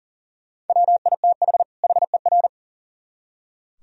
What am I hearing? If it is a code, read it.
Morse code, heard as a keyed tone, about 52 wpm, 700 Hz. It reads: WITH HER